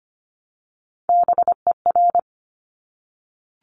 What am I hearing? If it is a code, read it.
Morse code, heard as a keyed tone, about 25 wpm, 700 Hz. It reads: BER
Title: Top Three Letter Combinations in Words 25wpm